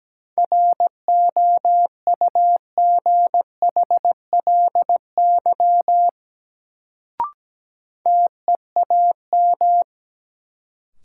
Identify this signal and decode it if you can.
Morse code, heard as a keyed tone, about 17 wpm, 700 Hz. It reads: ROUGHLY TEAM
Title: Sentences from Top 1500 Words - New Word Drill - 17wpm